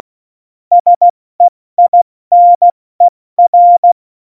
Morse code, heard as a keyed tone, about 16 wpm, 700 Hz. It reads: SEINER